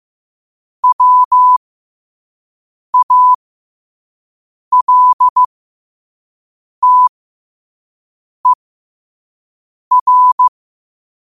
Morse code, heard as a keyed tone, about 15 wpm, spaced as for 5 wpm, 1000 Hz. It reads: WALTER
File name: qotdfiles / 5wpm